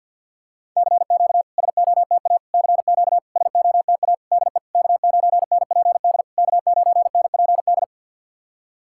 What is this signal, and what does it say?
Morse code, heard as a keyed tone, about 51 wpm, 690 Hz. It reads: CQ SOTA CQ SOTA DE K9NPD K9NPD
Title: SOTA QSO Practice 50wpm